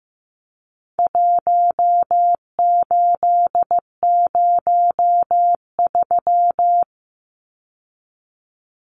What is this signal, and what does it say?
Morse code, heard as a keyed tone, about 15 wpm, 700 Hz. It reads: 1803
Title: Numbers - 4-Digit 15wpm